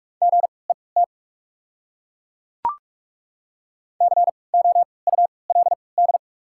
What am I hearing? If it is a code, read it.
Morse code, heard as a keyed tone, about 43 wpm, 700 Hz. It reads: G E T C O U L D
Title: 3x Character Spacing - Top 500 Words - 45wpm